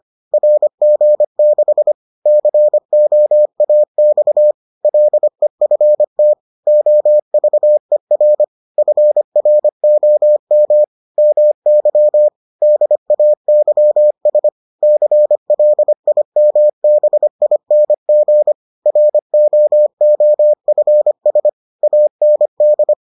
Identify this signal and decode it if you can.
Morse code, heard as a keyed tone, about 25 wpm, 600 Hz. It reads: RG6 COAX LEFT OVER FROM MY DAYS CLIMBING ROOFS AND